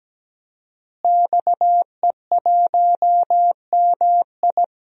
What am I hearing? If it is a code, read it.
Morse code, heard as a keyed tone, about 17 wpm, 700 Hz. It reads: XE1MI
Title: DXCC-100 Call Signs - 5 Characters - Repeated 2x - 17wpm